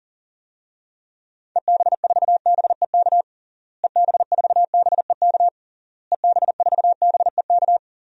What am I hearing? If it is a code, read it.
Morse code, heard as a keyed tone, about 40 wpm, 700 Hz. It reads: EB4BEK EB4BEK EB4BEK
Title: DXCC-100 Call Signs - 6 Characters - Repeated 3x - 40wpm